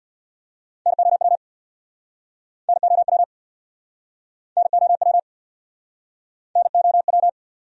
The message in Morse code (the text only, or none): NOW NOW NOW NOW